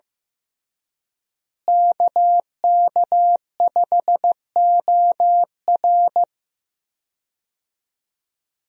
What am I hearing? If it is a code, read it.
Morse code, heard as a keyed tone, about 15 wpm, 700 Hz. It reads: KK5OR
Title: USA Call Signs - 5 Characters - Repeated 3x - 15wpm